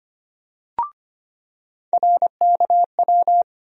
Morse code, heard as a keyed tone, about 25 wpm, 700 Hz. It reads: RKW
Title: USA Call Sign Postfixes - 3 Letters 25wpm